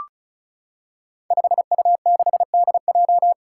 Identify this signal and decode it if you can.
Morse code, heard as a keyed tone, about 36 wpm, 700 Hz. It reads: 5U6DJ
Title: Five Challenging Letter-Number Combinations 35wpm